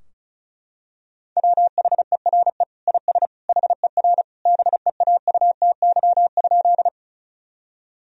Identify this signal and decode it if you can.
Morse code, heard as a keyed tone, about 36 wpm, 700 Hz. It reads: WHERE IS HER BEAUTY?
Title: Sentences from Top 1000 Words - Review Part 2 - 35wpm